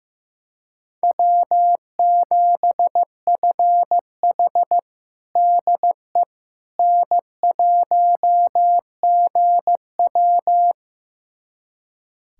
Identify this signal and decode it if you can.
Morse code, heard as a keyed tone, about 15 wpm, 700 Hz. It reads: W7FH DE N1GW
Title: USA Callsign DE Callsign - 4 Characters - 15wpm